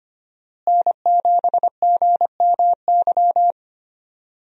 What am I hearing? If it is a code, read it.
Morse code, heard as a keyed tone, about 25 wpm, 700 Hz. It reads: N7GMY
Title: USA Call Signs - 5 Characters - Repeated 3x - 25wpm